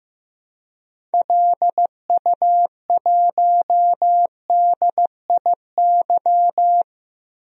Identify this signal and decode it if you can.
Morse code, heard as a keyed tone, about 15 wpm, 700 Hz. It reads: LU1DIY
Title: DXCC-100 Call Signs - 6 Characters - Repeated 3x - 15wpm